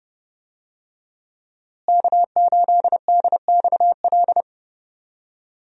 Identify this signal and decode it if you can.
Morse code, heard as a keyed tone, about 30 wpm, 700 Hz. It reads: K8DXL
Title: USA Call Signs - 5 Characters - Repeated 3x - 30wpm